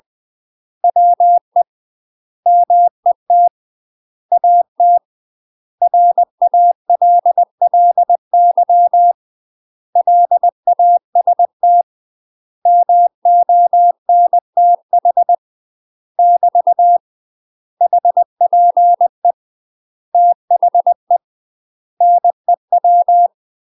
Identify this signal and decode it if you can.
Morse code, heard as a keyed tone, about 20 wpm, 700 Hz. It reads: WE MET AT RALLY LAST MONTH <BT> HPE THE NEW
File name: UK MASTER QSOs TRAINING / 20WPM / UK MASTER QSOs 0041